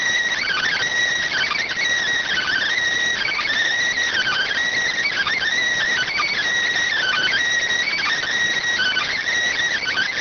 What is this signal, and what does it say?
Emergency Alert System (SAME) tones (header bursts), heard as harsh buzzing bursts. It is heard from 0.4 to 9.2 s.